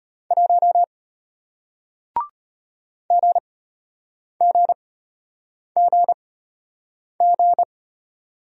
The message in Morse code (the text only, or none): 1 GGGG